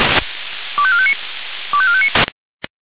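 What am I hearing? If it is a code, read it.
Five-tone selective call (ZVEI), heard as quick runs of five tones, tones of about 70 ms. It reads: 25459 25459
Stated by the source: ZVEI five-tone call 25897 received by SDR.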